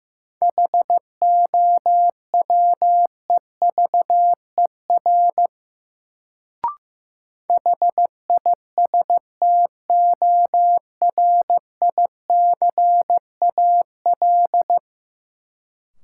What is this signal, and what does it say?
Morse code, heard as a keyed tone, about 15 wpm, 700 Hz. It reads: HOWEVER HISTORICAL
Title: Sentences from Top 600 Words - New Word Drill - 15wpm